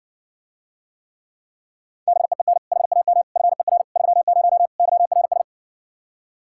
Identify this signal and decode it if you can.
Morse code, heard as a keyed tone, about 60 wpm, 700 Hz. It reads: BEEN HAM FER 30 YRS